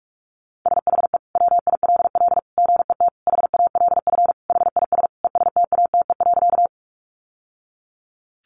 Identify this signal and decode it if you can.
Morse code, heard as a keyed tone, about 44 wpm, 700 Hz. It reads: SHE WILL GET HALF HIS ESTATE.
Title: Sentences from Top 1200 Words 45wpm